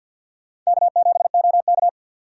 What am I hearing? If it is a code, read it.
Morse code, heard as a keyed tone, about 62 wpm, 700 Hz. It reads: K7OK
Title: Speed Racing - USA 1x2 Call Signs 50wpm